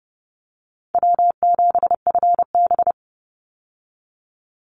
Morse code, heard as a keyed tone, about 30 wpm, 700 Hz. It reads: W7FB
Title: Call Signs - 30wpm